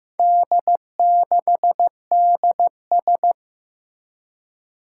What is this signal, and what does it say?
Morse code, heard as a keyed tone, about 15 wpm, 700 Hz. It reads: D6DS
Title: Four Challenging Letter-Number Combinations 15wpm